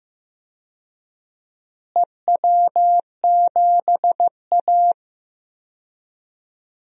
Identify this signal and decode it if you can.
Morse code, heard as a keyed tone, about 15 wpm, 700 Hz. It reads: EW7A
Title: DXCC-100 Call Signs - 4 Characters - Repeated 2x - 15wpm